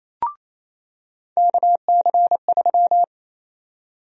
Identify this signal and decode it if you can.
Morse code, heard as a keyed tone, about 28 wpm, 700 Hz. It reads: KC3
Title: USA Call Sign Prefixes - 28wpm